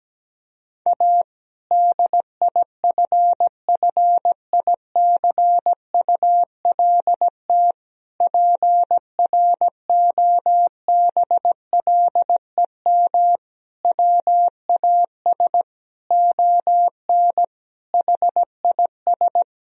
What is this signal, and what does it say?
Morse code, heard as a keyed tone, about 17 wpm, 700 Hz. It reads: A DIFFICULT PROBLEM WAS ON HIS
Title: Sentences from Top 1000 Words - Review Part 2 - 17wpm